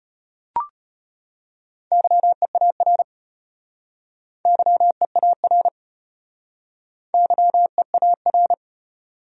Morse code, heard as a keyed tone, about 35 wpm, 700 Hz. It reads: YEAR YEAR YEAR